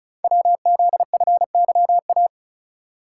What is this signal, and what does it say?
Morse code, heard as a keyed tone, about 36 wpm, 690 Hz. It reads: WZFYA
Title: Code Groups - 5 Letters - 35wpm